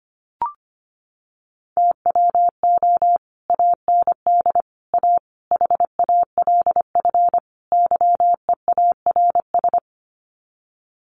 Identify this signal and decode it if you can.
Morse code, heard as a keyed tone, about 25 wpm, 700 Hz. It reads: TWO AND A HALF YEARS
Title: Sets of 5 Words 25wpm